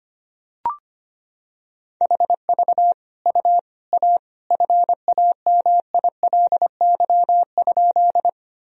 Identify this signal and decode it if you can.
Morse code, heard as a keyed tone, about 25 wpm, 700 Hz. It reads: HV U A FAMILY?
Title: Sentences from Top 400 Words and Abbreviations 25wpm